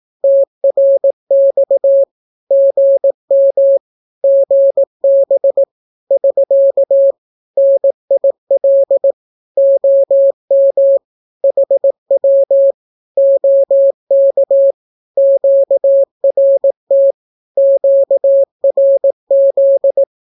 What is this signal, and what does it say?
Morse code, heard as a keyed tone, about 18 wpm, 550 Hz. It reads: TRX GM GB <SK> NIL OM HW OK QRT QRZ
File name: naukapodstaw / skroty / skroty